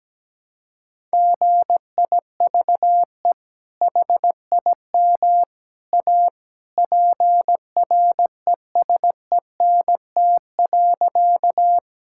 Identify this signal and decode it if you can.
Morse code, heard as a keyed tone, about 17 wpm, 700 Hz. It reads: GIVE HIM A PRESENT.
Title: Sentences from Top 1000 Words - Review Part 5 - 17wpm